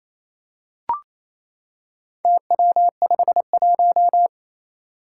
Morse code, heard as a keyed tone, about 28 wpm, 700 Hz. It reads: TW51
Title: Four Challenging Letter-Number Combinations 28wpm